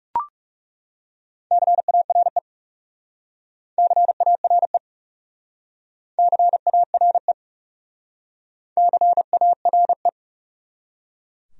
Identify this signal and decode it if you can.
Morse code, heard as a keyed tone, about 36 wpm, 700 Hz. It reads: CARE CARE CARE CARE
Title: Speed-Racing Top 300 Words 30wpm